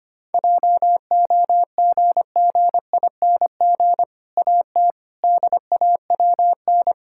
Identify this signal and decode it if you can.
Morse code, heard as a keyed tone, about 25 wpm, 700 Hz. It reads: JOGGING AT DAWN
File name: practice / 25wpm / Difficult Sentences with Hints - 25wpm